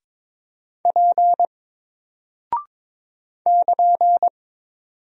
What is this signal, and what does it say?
Morse code, heard as a keyed tone, about 22 wpm, 700 Hz. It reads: P <KN>
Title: Single Character (Letter:Number:Punctuation:Prosign) Alt 22wpm